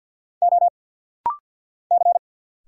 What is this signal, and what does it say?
Morse code, heard as a keyed tone, about 50 wpm, 700 Hz. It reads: O C
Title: Single Letters - Rapid-Fire 50wpm